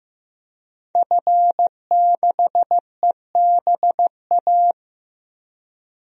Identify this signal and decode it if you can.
Morse code, heard as a keyed tone, about 15 wpm, 700 Hz. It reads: F6EBA
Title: DXCC-100 Call Signs - 5 Characters - Repeated 3x - 15wpm